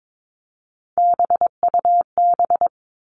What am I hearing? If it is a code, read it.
Morse code, heard as a keyed tone, about 22 wpm, 700 Hz. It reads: BUB
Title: All Cut Numbers - 3-Digit 22wpm